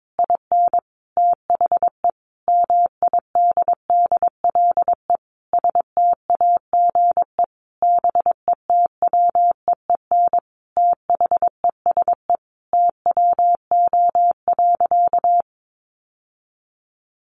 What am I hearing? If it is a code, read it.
Morse code, heard as a keyed tone, about 22 wpm, 700 Hz. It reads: IN THE MIDDLE STAGE BETWEEN THESE TWO.
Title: Sentences from Top 1200 Words 22wpm